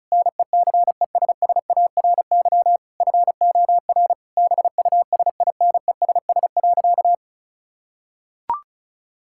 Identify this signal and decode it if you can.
Morse code, heard as a keyed tone, about 34 wpm, 700 Hz. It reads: NECESSARY FOR BUSINESS.
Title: Sentences from Top 1000 Words - Review Part 5 - 35wpm